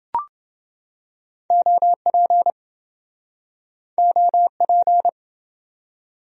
Morse code, heard as a keyed tone, about 24 wpm, 700 Hz. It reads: OP OP
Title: Speed-Racing Abbreviations - 20wpm